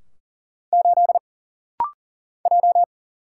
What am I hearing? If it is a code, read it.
Morse code, heard as a keyed tone, about 40 wpm, 700 Hz. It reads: 8 J
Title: Single Letter-Number - Rapid-Fire 40wpm